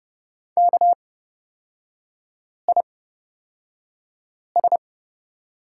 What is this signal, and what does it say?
Morse code, heard as a keyed tone, about 29 wpm, spaced as for 5 wpm, 700 Hz. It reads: KIS
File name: Lesson 37e - Letter x - Three Characters 30wpm